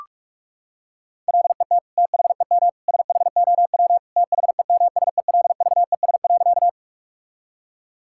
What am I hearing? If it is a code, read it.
Morse code, heard as a keyed tone, about 46 wpm, 700 Hz. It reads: LET THEM SHOW THEMSELVES.
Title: Sentences from Top 1000 Words - Review Part 5 - 45wpm